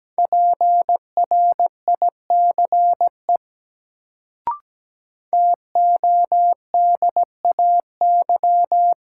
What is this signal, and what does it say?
Morse code, heard as a keyed tone, about 17 wpm, 700 Hz. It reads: PRICE TODAY
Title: Sentences from Top 600 Words - New Word Drill - 17wpm